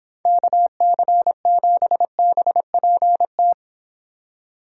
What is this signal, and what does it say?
Morse code, heard as a keyed tone, about 29 wpm, 700 Hz. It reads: KC7BPT